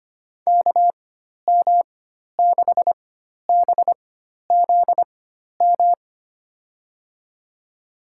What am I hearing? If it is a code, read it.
Morse code, heard as a keyed tone, about 25 wpm, spaced as for 12 wpm, 700 Hz. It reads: KM6BZM
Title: USA Call Signs - 6 Characters - 4x Character Spacing - 25wpm